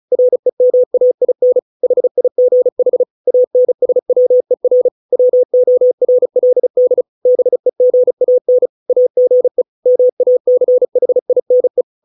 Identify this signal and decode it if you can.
Morse code, heard as a keyed tone, about 34 wpm, 500 Hz. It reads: REMAIN HIGH ANSWER WORLD BEGAN AGE MACHINE